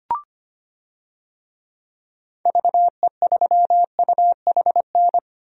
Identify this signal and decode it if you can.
Morse code, heard as a keyed tone, about 25 wpm, 700 Hz. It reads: VE3UHN
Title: DXCC-100 Call Signs - 6 Characters - Repeated 2x - 25wpm